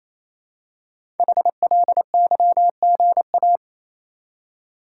Morse code, heard as a keyed tone, about 27 wpm, 700 Hz. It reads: HLYGA
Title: Code Groups - 5 Letters - 28wpm